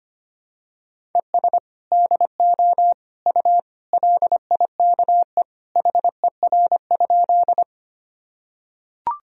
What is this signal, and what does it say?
Morse code, heard as a keyed tone, about 25 wpm, 700 Hz. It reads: ES DO U LIKE HER?
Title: Sentences from Top 100 Words and Abbreviations 25wpm